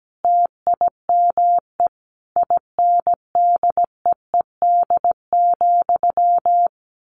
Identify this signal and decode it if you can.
Morse code, heard as a keyed tone, about 17 wpm, 700 Hz. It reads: TIME INDEED,
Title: A Sound Of Thunder 17wpm